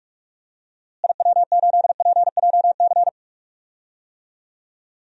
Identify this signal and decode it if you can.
Morse code, heard as a keyed tone, about 46 wpm, 700 Hz. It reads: IW8PJC
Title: DXCC-100 Call Signs - 6 Characters - Repeated 2x - 45wpm